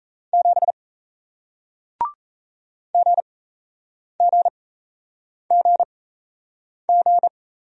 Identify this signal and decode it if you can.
Morse code, heard as a keyed tone, about 38 wpm, spaced as for 8 wpm, 700 Hz. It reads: 7 GGGG